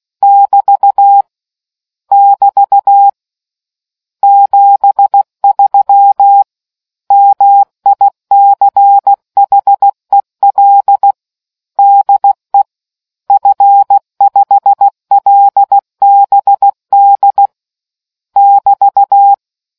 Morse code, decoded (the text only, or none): <BT> <BT> 73 MICHEL DE F5LBD <BT>